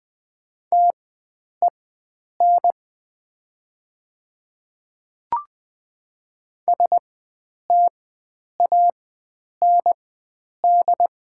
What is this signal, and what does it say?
Morse code, heard as a keyed tone, about 20 wpm, spaced as for 9 wpm, 700 Hz. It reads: TEN STAND
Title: 4x Character Spacing - Top 500 Words - 20wpm